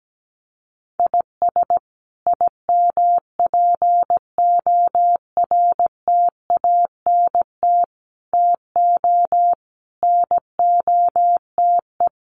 Morse code, heard as a keyed tone, about 17 wpm, 700 Hz. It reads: IS IMPORTANT TO NOTE
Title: Sets of 5 Words 17wpm